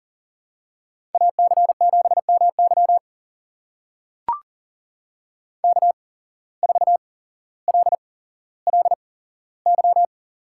Morse code, heard as a keyed tone, about 40 wpm, spaced as for 17 wpm, 700 Hz. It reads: AC7MY K4LLY
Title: USA Call Signs - 5 Characters - 8x Character Spacing - 40wpm